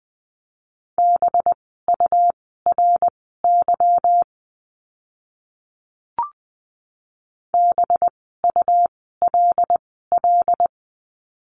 Morse code, heard as a keyed tone, about 20 wpm, spaced as for 14 wpm, 700 Hz. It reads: BURY BULL
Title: 185 Lesson 30f - Letter b - Words 20wpm